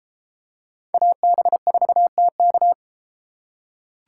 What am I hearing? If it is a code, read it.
Morse code, heard as a keyed tone, about 31 wpm, 700 Hz. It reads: AB4TK